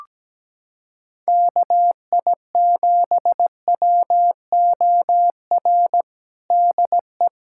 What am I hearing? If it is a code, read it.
Morse code, heard as a keyed tone, about 17 wpm, 700 Hz. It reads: KI7WOR DE
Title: USA Callsign DE Callsign - 6 Characters - 17wpm